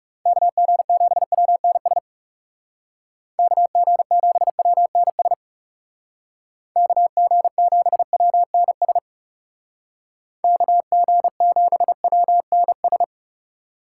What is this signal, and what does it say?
Morse code, heard as a keyed tone, about 36 wpm, 700 Hz. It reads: KG7WNS KG7WNS KG7WNS KG7WNS
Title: Speed Racing - USA 2x3 Call Signs 30wpm